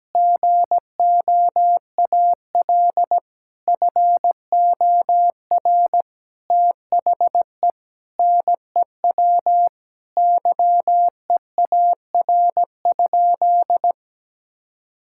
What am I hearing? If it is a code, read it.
Morse code, heard as a keyed tone, about 17 wpm, 700 Hz. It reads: GOAL FOR THE NEW YEAR?
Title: Sentences from Top 1000 Words - Review Part 5 - 17wpm